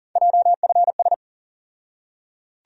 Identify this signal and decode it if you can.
Morse code, heard as a keyed tone, about 40 wpm, 700 Hz. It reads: JFS
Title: USA Call Sign Postfixes - 3 Letters 40wpm